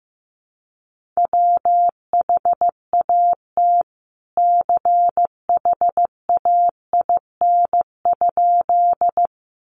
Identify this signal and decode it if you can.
Morse code, heard as a keyed tone, about 15 wpm, 700 Hz. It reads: WHAT CHAIN?
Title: Sentences from Top 1500 Words 15wpm